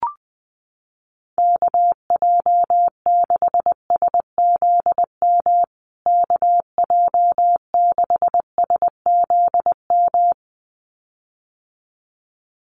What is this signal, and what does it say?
Morse code, heard as a keyed tone, about 20 wpm, 700 Hz. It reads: KJ6SZM KJ6SZM
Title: USA Call Signs Repeated 2x 20wpm